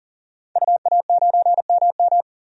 Morse code, heard as a keyed tone, about 40 wpm, 700 Hz. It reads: UA9MM
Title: DXCC-100 Call Signs - 5 Characters - 40wpm